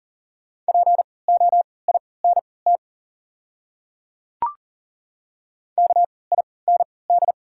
Morse code, heard as a keyed tone, about 39 wpm, spaced as for 23 wpm, 690 Hz. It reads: POINT KIND
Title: 3x Character Spacing - Top 300 Words - 40wpm